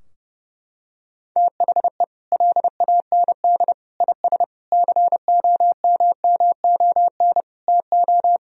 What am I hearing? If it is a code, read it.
Morse code, heard as a keyed tone, about 30 wpm, 700 Hz. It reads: THE LAND IS COMMON TO
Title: Sentences from Top 1000 Words - Review Part 3 - 30wpm